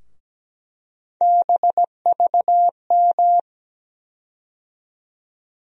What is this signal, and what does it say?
Morse code, heard as a keyed tone, about 17 wpm, 700 Hz. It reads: BVM
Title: USA Call Sign Postfixes - 3 Letters 17wpm